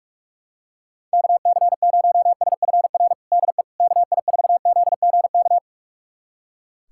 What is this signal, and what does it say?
Morse code, heard as a keyed tone, about 41 wpm, 700 Hz. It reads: KC0SFR DE KI4ZGK
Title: USA Callsign DE Callsign - 6 Characters - 45wpm